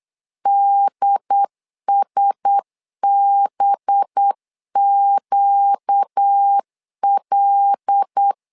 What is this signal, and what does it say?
Morse code, heard as a keyed tone, about 8 wpm, 800 Hz. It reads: DSBQL
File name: LEZIONI / 5 LBVC / 5 LB 8wpm 2816